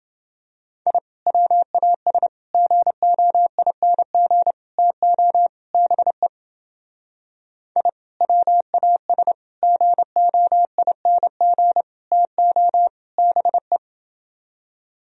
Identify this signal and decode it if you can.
Morse code, heard as a keyed tone, about 29 wpm, 700 Hz. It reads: I WAS GOING TO BE I WAS GOING TO BE